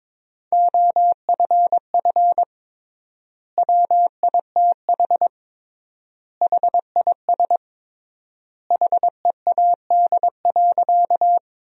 Morse code, heard as a keyed tone, about 22 wpm, 700 Hz. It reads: OFF WITH HIS HEAD.
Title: 3x Word-Spacing - Sentences from Top 300 Words - 22wpm